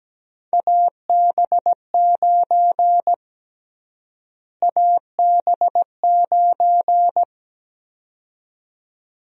Morse code, heard as a keyed tone, about 17 wpm, 700 Hz. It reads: AB9 AB9
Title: USA Call Sign Prefixes - Repeated 2x - 17wpm